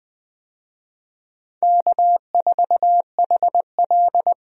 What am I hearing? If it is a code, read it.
Morse code, heard as a keyed tone, about 20 wpm, 700 Hz. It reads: K4HL
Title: USA Call Signs - 4 Characters - Repeated 3x - 20wpm